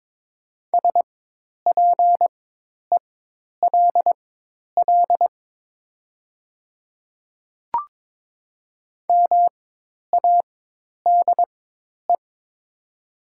Morse code, heard as a keyed tone, about 22 wpm, spaced as for 10 wpm, 700 Hz. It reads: SPELL MADE
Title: 4x Character Spacing - Top 200 Words - 22wpm